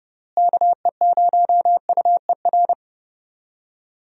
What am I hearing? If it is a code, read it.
Morse code, heard as a keyed tone, about 32 wpm, 700 Hz. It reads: KE0UER
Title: USA Call Signs - 6 Characters - Repeated 3x - 30wpm